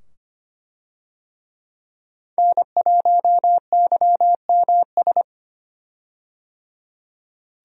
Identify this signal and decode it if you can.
Morse code, heard as a keyed tone, about 25 wpm, 700 Hz. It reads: N1YMS
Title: USA Call Signs - 5 Characters - 25wpm